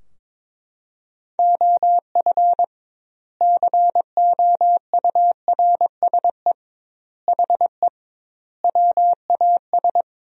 Morse code, heard as a keyed tone, about 22 wpm, 700 Hz. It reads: OF COURSE HE WAS
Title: 2x Word-Spacing - Sentences from Top 500 Words - 22wpm